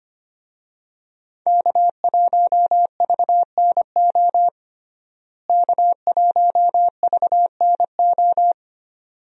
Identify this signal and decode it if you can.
Morse code, heard as a keyed tone, about 25 wpm, 700 Hz. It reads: K1VNO K1VNO
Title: USA Call Signs - 5 Characters - Repeated 2x - 25wpm